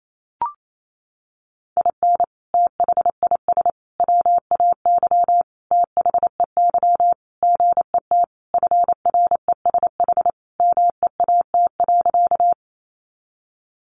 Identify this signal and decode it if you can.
Morse code, heard as a keyed tone, about 28 wpm, 700 Hz. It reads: IN THIS WAY THEY GET FRESH MEAT.
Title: Sentences from Top 1200 Words 28wpm